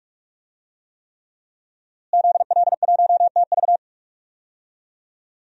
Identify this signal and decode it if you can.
Morse code, heard as a keyed tone, about 42 wpm, 700 Hz. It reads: ZL1TV
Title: DXCC-100 Call Signs - 5 Characters - Repeated 3x - 45wpm